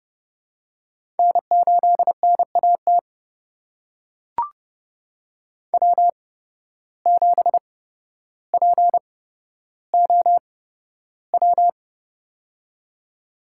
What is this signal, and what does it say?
Morse code, heard as a keyed tone, about 30 wpm, spaced as for 13 wpm, 700 Hz. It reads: N8NAT W7POW
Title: USA Call Signs - 8x Character Spacing - 30wpm